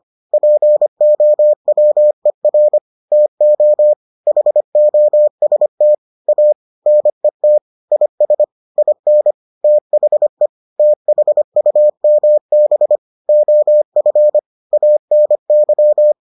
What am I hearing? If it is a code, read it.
Morse code, heard as a keyed tone, about 25 wpm, 600 Hz. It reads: POWER TO HOST A NET IS IN THE THUMB OF ANY